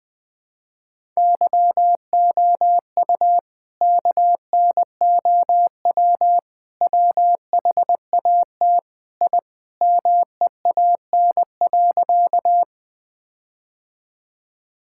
Morse code, heard as a keyed tone, about 21 wpm, 700 Hz. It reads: YOU KNOW WHAT I MEAN.
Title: Sentences with 4-Word Phrases - 20wpm